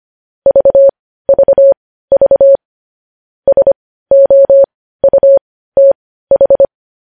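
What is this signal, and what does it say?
Morse code, heard as a keyed tone, about 25 wpm, spaced as for 15 wpm, 550 Hz. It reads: VVV SOUTH
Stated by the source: Morse news headlines practice at 15.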